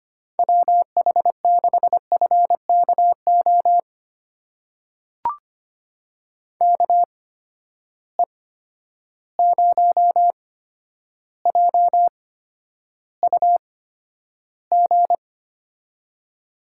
Morse code, heard as a keyed tone, about 25 wpm, spaced as for 11 wpm, 700 Hz. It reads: WH6FKO KE0JUG
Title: USA Call Signs - 6 Characters - 8x Character Spacing - 25wpm